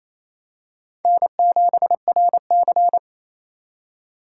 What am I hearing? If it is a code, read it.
Morse code, heard as a keyed tone, about 29 wpm, 700 Hz. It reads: N7RC
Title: USA Call Signs - 3x Character Spacing - 28wpm